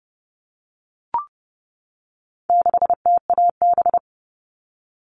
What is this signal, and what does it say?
Morse code, heard as a keyed tone, about 30 wpm, 700 Hz. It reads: 6TAB
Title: All Cut Numbers - 4-Digit 30wpm